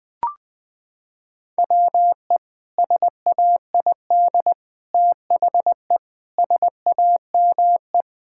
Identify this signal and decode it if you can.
Morse code, heard as a keyed tone, about 20 wpm, 700 Hz. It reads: WE SAID THE SAME
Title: Sentences with 3-Word Phrases - 20wpm